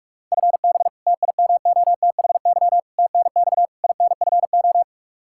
Morse code, heard as a keyed tone, about 46 wpm, 700 Hz. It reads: FB TIMOTHY TNX INFO
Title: K1USN SST QSO Practice 45wpm